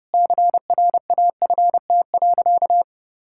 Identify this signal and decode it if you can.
Morse code, heard as a keyed tone, about 32 wpm, 700 Hz. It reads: CRAFT.
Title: Sentences from Top 1000 Words - Review Part 6 - 30wpm